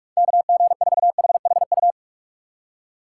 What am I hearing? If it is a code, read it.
Morse code, heard as a keyed tone, about 65 wpm, spaced as for 55 wpm, 700 Hz. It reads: KG4HHU
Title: Speed Racing - USA 2x3 Call Signs 45wpm